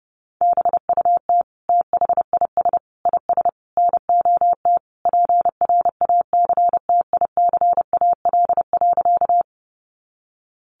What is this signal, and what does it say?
Morse code, heard as a keyed tone, about 30 wpm, 700 Hz. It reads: BUT THIS IS NOT PRACTICAL.
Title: Sentences from Top 1400 Words 30wpm